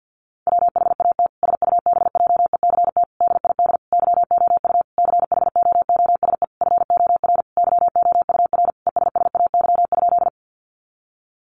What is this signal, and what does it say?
Morse code, heard as a keyed tone, about 50 wpm, 700 Hz. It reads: WHAT SUBJECT DID YOU CHOOSE FOR YOUR ESSAY?
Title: Sentences from Top 1500 Words 50wpm